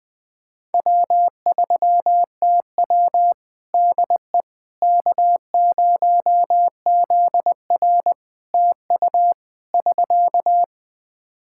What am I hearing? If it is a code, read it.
Morse code, heard as a keyed tone, about 21 wpm, 700 Hz. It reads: W3TW DE K0ZR TU <SK>
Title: K1USN SST QSO Practice 20wpm